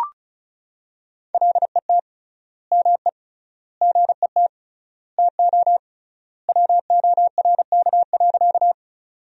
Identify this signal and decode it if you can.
Morse code, heard as a keyed tone, about 36 wpm, 700 Hz. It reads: LET ME GET TO WORK.
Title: 3x Word-Spacing - Sentences from Top 300 Words - 35wpm